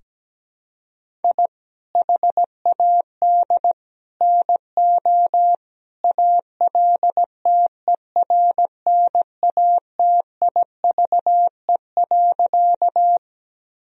Morse code, heard as a keyed tone, about 17 wpm, 700 Hz. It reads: I HAD NO ALTERNATIVE.
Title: Sentences from Top 1000 Words - Review Part 4 - 17wpm